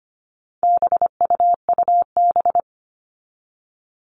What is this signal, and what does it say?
Morse code, heard as a keyed tone, about 25 wpm, 700 Hz. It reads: BUUB
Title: All Cut Numbers - 4-Digit 25wpm